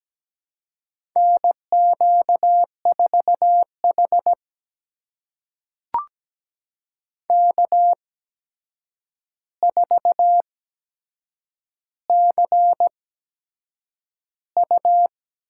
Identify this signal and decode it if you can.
Morse code, heard as a keyed tone, about 17 wpm, spaced as for 7 wpm, 700 Hz. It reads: NQ4H K4CU
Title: USA Call Signs - 4 Characters - 8x Character Spacing - 17wpm